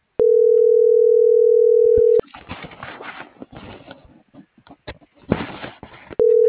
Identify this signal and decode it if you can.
Ringback tone (North America), 440 + 480 Hz.
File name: CENTRAL SKYBLUE / out axel.juarez